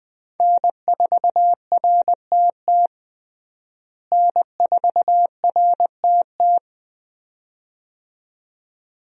Morse code, heard as a keyed tone, about 20 wpm, 700 Hz. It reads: N4RTT N4RTT
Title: USA Call Signs - 5 Characters - Repeated 2x - 20wpm